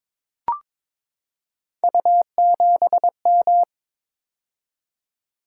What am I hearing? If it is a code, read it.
Morse code, heard as a keyed tone, about 22 wpm, 700 Hz. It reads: U7M
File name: practice / 22wpm / Three Letter-Number Combinations - 22wpm